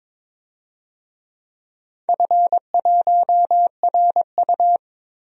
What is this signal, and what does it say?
Morse code, heard as a keyed tone, about 22 wpm, 700 Hz. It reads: F1RU
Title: DXCC-100 Call Signs - 4 Characters - Repeated 3x - 22wpm